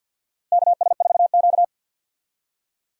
Morse code, heard as a keyed tone, about 51 wpm, 700 Hz. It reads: KS4Q
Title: USA Call Signs - 4 Characters - 50wpm